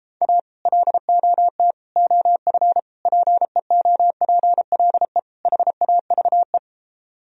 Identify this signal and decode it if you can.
Morse code, heard as a keyed tone, about 33 wpm, 700 Hz. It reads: A LOT OF PEOPLE HAVE